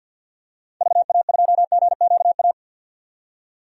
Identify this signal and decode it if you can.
Morse code, heard as a keyed tone, about 48 wpm, 700 Hz. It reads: VA2GQA
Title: DXCC-100 Call Signs - 6 Characters - 50wpm